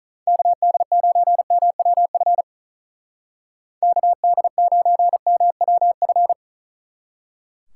Morse code, heard as a keyed tone, about 43 wpm, 700 Hz. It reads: KD9MWF KD9MWF
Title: Speed Racing - USA 2x3 Call Signs 35wpm